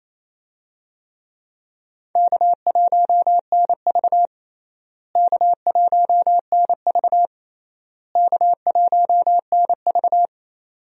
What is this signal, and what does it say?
Morse code, heard as a keyed tone, about 28 wpm, 700 Hz. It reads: K1NV K1NV K1NV
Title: USA Call Signs - 4 Characters - Repeated 3x - 28wpm